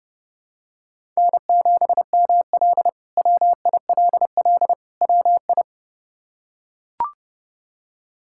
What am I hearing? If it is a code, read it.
Morse code, heard as a keyed tone, about 29 wpm, 700 Hz. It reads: N7ML WILL WI
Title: K1USN SST QSO Practice - 30wpm